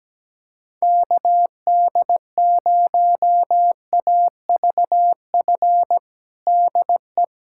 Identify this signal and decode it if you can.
Morse code, heard as a keyed tone, about 17 wpm, 700 Hz. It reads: KD0AVF DE
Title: USA Callsign DE Callsign 17wpm